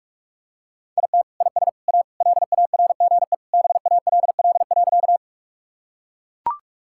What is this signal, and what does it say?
Morse code, heard as a keyed tone, about 42 wpm, 690 Hz. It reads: IT IS A LARGE BALL.
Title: Sentences from Top 1000 Words - Review Part 6 - 45wpm